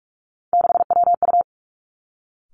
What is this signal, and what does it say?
Morse code, heard as a keyed tone, about 46 wpm, 700 Hz. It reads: 6WU
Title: Three Challenging Letter-Number Combinations 45wpm